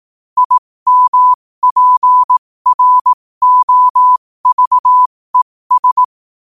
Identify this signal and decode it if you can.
Morse code, heard as a keyed tone, about 18 wpm, spaced as for 15 wpm, 1000 Hz. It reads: IMPROVES
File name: qotdfiles / 15wpm